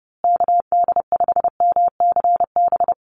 Morse code, heard as a keyed tone, about 30 wpm, 700 Hz. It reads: KD5MCB
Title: USA Call Signs Repeated 3x 30wpm